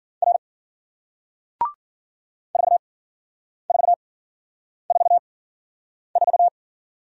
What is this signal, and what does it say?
Morse code, heard as a keyed tone, about 54 wpm, spaced as for 9 wpm, 700 Hz. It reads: R 4444